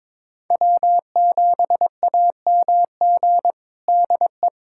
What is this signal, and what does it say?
Morse code, heard as a keyed tone, about 21 wpm, 700 Hz. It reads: W7AMG DE
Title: USA Callsign DE Callsign - 5 Characters - 22wpm